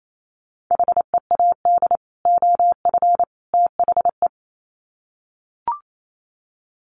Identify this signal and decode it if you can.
Morse code, heard as a keyed tone, about 28 wpm, 700 Hz. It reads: HEAD OF THE
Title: Sets of 3 Words 28wpm